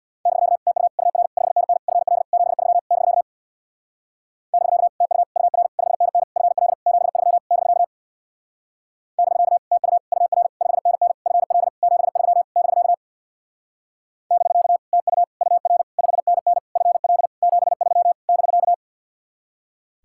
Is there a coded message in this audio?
Yes: Morse code from 0.3 to 18.7 s, heard as a keyed tone, about 64 wpm, 700 Hz. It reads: <BK> TU UR 5NN FL 73 <BK> <BK> TU UR 5NN FL 73 <BK> <BK> TU UR 5NN FL 73 <BK> <BK> TU UR 5NN FL 73 <BK>